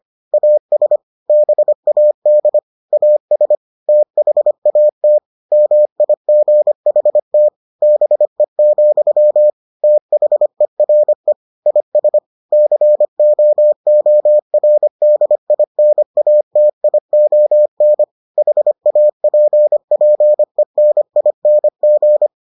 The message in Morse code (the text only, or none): AS BAD AS THAT MIGHT BE, THERE IS COORDINATION HAPPENING